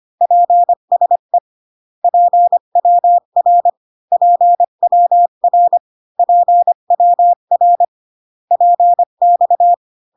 Morse code, heard as a keyed tone, about 25 wpm, 700 Hz. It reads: PSE PWR PWR PWR PX